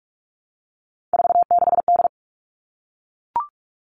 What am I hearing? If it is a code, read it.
Morse code, heard as a keyed tone, about 44 wpm, 700 Hz. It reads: 46D
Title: All Cut Numbers - 3-Digit 45wpm